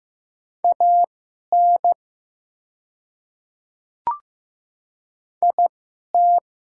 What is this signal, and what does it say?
Morse code, heard as a keyed tone, about 15 wpm, spaced as for 11 wpm, 700 Hz. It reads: AN IT
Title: 2x Character Spacing - Top 100 Words - 15wpm